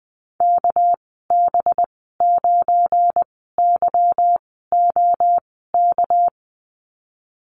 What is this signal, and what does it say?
Morse code, heard as a keyed tone, about 20 wpm, spaced as for 14 wpm, 700 Hz. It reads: KB9YOK
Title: 207 Lesson 33g - Letter j - Call Signs 20wpm